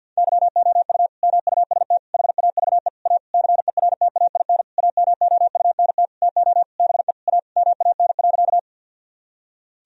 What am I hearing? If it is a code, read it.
Morse code, heard as a keyed tone, about 50 wpm, 700 Hz. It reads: YOU MUST HAVE A CERTAIN AMOUNT TO BE A MAN.
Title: Sentences from Top 1000 Words - Review Part 4 - 50wpm